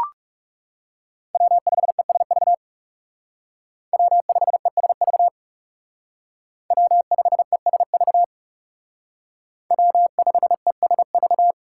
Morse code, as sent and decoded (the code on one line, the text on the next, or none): .-- ..... ... ...- / .-- ..... ... ...- / .-- ..... ... ...- / .-- ..... ... ...-
W5ESV W5ESV W5ESV W5ESV